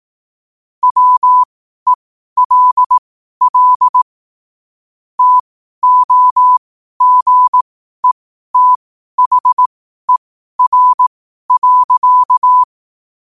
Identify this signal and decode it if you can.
Morse code, heard as a keyed tone, about 18 wpm, spaced as for 12 wpm, 1000 Hz. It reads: WELL TOGETHER.
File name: qotdfiles / 12wpm